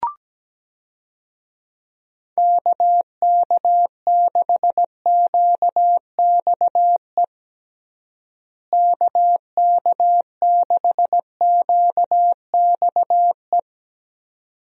Morse code, heard as a keyed tone, about 17 wpm, 700 Hz. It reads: KK6QXE KK6QXE
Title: USA Call Signs - 6 Characters - Repeated 2x - 17wpm